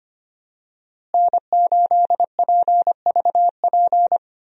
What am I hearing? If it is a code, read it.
Morse code, heard as a keyed tone, about 25 wpm, 700 Hz. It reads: N8PVP
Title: USA Call Signs - 5 Characters - Repeated 3x - 25wpm